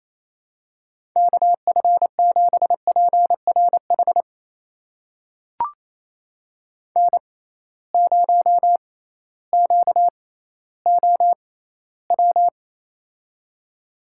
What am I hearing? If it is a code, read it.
Morse code, heard as a keyed tone, about 28 wpm, spaced as for 15 wpm, 700 Hz. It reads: KF7PRH N0QOW
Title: USA Call Signs - 6x Character Spacing - 28wpm